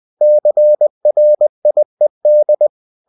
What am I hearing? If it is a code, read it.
Morse code, heard as a keyed tone, about 20 wpm, 600 Hz. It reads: CRIED
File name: cw-books / winnie-the-pooh / winnie-20wpm-page-0073